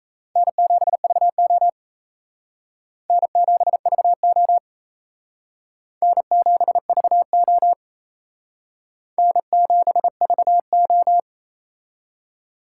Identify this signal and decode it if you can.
Morse code, heard as a keyed tone, about 37 wpm, 700 Hz. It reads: N7VO N7VO N7VO N7VO